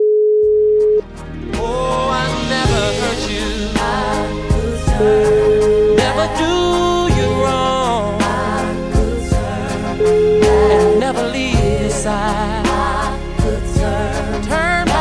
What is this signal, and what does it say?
Ringback tone (Europe, 425 Hz), with noise.